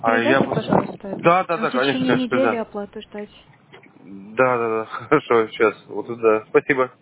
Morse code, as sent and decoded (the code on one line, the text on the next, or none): none